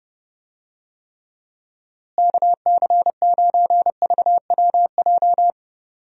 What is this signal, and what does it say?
Morse code, heard as a keyed tone, about 30 wpm, 700 Hz. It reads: KC9VWJ
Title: USA Call Signs - 6 Characters - Repeated 3x - 30wpm